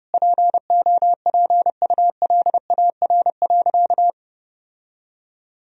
Morse code, heard as a keyed tone, about 30 wpm, 700 Hz. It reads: POPULAR.
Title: Sentences from Top 1000 Words - Review Part 3 - 30wpm